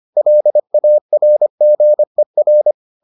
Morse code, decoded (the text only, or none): LARGER